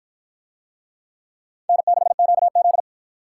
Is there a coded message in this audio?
Yes: Morse code from 1.7 to 2.8 s, heard as a keyed tone, about 44 wpm, 700 Hz. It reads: N6QZ